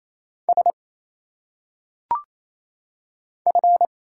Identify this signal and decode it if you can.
Morse code, heard as a keyed tone, about 27 wpm, 700 Hz. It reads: S F
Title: Single Letters - Phonetic Pronunciation 28wpm